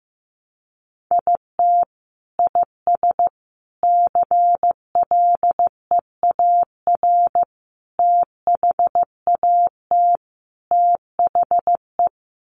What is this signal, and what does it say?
Morse code, heard as a keyed tone, about 15 wpm, 700 Hz. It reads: IT IS CLEAR THAT THE
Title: Sets of 5 Words Encore 15wpm